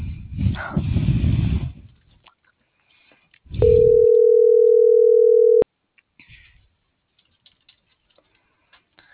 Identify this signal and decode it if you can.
Ringback tone (North America), 440 + 480 Hz.